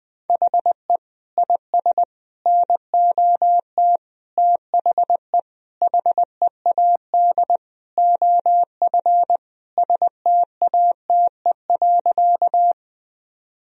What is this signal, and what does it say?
Morse code, heard as a keyed tone, about 20 wpm, 700 Hz. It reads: HE IS NOT THE HEAD OF STATE.
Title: Sentences from Top 1000 Words - Review Part 2 - 20wpm